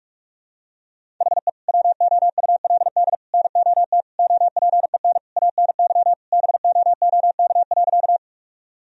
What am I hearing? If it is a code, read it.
Morse code, heard as a keyed tone, about 43 wpm, 700 Hz. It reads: HE WOULD NOT OPEN ANY BOOK.
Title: Sentences from Top 1000 Words - Review Part 1 - 45wpm